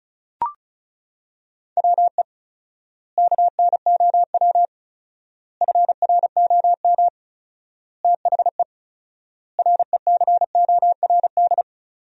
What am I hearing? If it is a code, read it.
Morse code, heard as a keyed tone, about 36 wpm, 700 Hz. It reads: WE KNOW FROM THE RECORD
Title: 4x Word-Spacing - Sentences from Top 500 Words - 35wpm